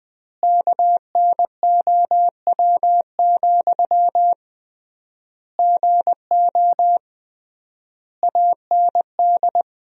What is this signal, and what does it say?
Morse code, heard as a keyed tone, about 20 wpm, 700 Hz. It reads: KNOW, GO AND
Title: 3x Word-Spacing - Sentences from Top 200 Words - 20wpm